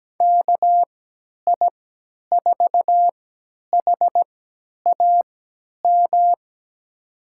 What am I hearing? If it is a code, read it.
Morse code, heard as a keyed tone, about 17 wpm, spaced as for 10 wpm, 700 Hz. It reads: KI4HAM
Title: USA Call Signs - 6 Characters - 3x Character Spacing - 17wpm